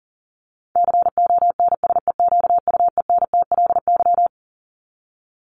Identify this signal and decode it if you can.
Morse code, heard as a keyed tone, about 40 wpm, 700 Hz. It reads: CONSEQUENTLY
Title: LY Suffix - Top 100 Words with Base Words 40wpm